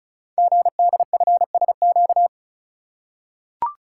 Morse code, heard as a keyed tone, about 36 wpm, 700 Hz. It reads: GDFSQ
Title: Code Groups - 5 Letters - 35wpm